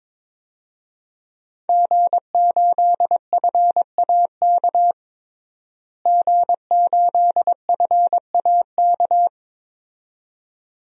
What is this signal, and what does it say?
Morse code, heard as a keyed tone, about 21 wpm, 700 Hz. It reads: G8FAK G8FAK
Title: DXCC-100 Call Signs - 5 Characters - Repeated 2x - 22wpm